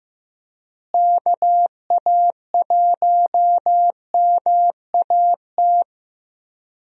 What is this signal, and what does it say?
Morse code, heard as a keyed tone, about 15 wpm, 700 Hz. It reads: KA1MAT
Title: USA Call Signs - 6 Characters - Repeated 3x - 15wpm